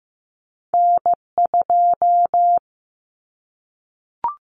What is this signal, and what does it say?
Morse code, heard as a keyed tone, about 15 wpm, 700 Hz. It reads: N2
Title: Callsign Prefixes for Top 25 DXCC - 15wpm